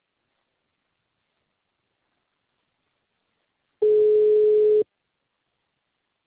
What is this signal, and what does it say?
Ringback tone (Europe, 425 Hz).
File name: out- -2003- - - .50561